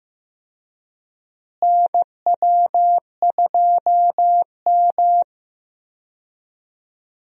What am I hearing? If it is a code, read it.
Morse code, heard as a keyed tone, about 15 wpm, 700 Hz. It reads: NW2M
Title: USA Call Signs - 4 Characters - Repeated 3x - 15wpm